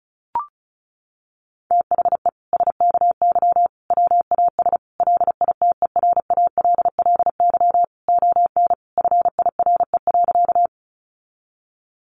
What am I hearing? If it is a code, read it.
Morse code, heard as a keyed tone, about 34 wpm, 700 Hz. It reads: THE SKY WAS LITERALLY ON FIRE.
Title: Sentences from Top 1400 Words 35wpm